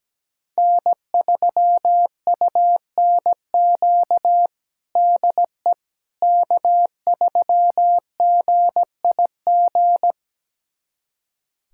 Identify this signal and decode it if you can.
Morse code, heard as a keyed tone, about 17 wpm, 700 Hz. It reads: N3UNQ DE K3GIG
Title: USA Callsign DE Callsign - 5 Characters - 17wpm